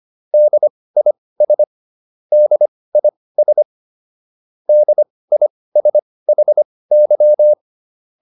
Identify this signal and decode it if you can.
Morse code, heard as a keyed tone, about 25 wpm, spaced as for 18 wpm, 600 Hz. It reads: DIS DIS DISHY